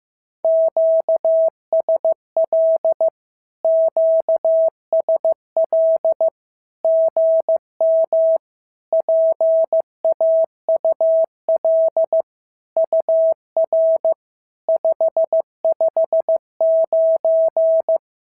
Morse code, heard as a keyed tone, about 15 wpm, 650 Hz. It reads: QSL QSL GM PAUL UR 559
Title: SOTA QSO Practice 15wpm